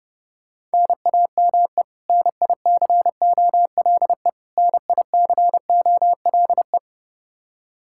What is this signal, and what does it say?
Morse code, heard as a keyed tone, about 30 wpm, 700 Hz. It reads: NAME NICOLE NICOLE
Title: Practice QSO - Exchange 1 - Beginning - 30wpm